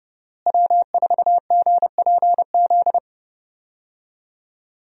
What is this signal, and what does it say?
Morse code, heard as a keyed tone, about 30 wpm, 700 Hz. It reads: W4GPZ
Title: USA Call Signs - 5 Characters - Repeated 3x - 30wpm